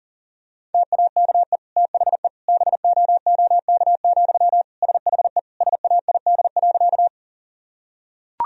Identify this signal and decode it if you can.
Morse code, heard as a keyed tone, about 37 wpm, 700 Hz. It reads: TAKE THE BOOK, SHE SAID.
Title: Sentences from Top 1000 Words - Review Part 1 - 40wpm